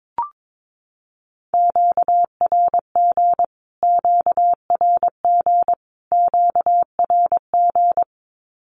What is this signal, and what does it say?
Morse code, heard as a keyed tone, about 22 wpm, 700 Hz. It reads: QRG QRG QRG
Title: Q-Codes Repeated 3x 22wpm